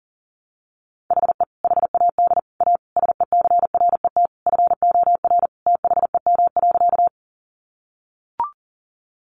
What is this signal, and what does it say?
Morse code, heard as a keyed tone, about 37 wpm, 700 Hz. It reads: HE HAD A SECRET FOR THEM.
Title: Sentences from Top 1200 Words 40wpm